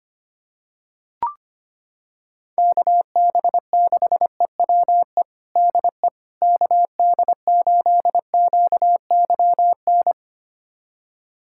Morse code, heard as a keyed tone, about 25 wpm, 700 Hz. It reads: KB6EWE DE KD8QYN
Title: USA Callsign DE Callsign - 6 Characters - 25wpm